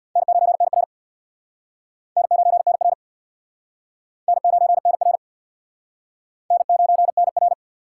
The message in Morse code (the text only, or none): N9NR N9NR N9NR N9NR